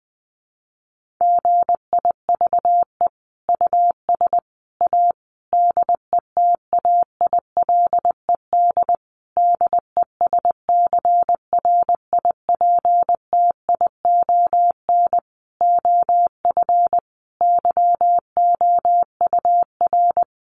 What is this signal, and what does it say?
Morse code, heard as a keyed tone, about 20 wpm, 700 Hz. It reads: GIVE US A DETAILED DESCRIPTION OF YOUR
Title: Sentences from Top 1500 Words 20wpm